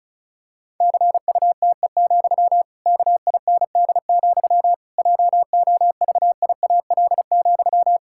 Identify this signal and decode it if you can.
Morse code, heard as a keyed tone, about 34 wpm, 700 Hz. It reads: CUTE, KIND, JOVIAL,
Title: Pangram Challenge v2 - 35wpm